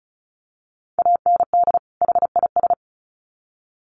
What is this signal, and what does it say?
Morse code, heard as a keyed tone, about 34 wpm, 700 Hz. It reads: AND HIS
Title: Sets of 2 Words 35wpm